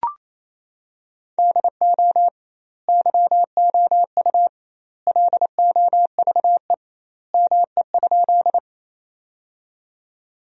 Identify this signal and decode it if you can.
Morse code, heard as a keyed tone, about 27 wpm, 700 Hz. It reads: DO YOU LOVE ME?
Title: 2x Word-Spacing - Sentences from Top 400 Words - 28wpm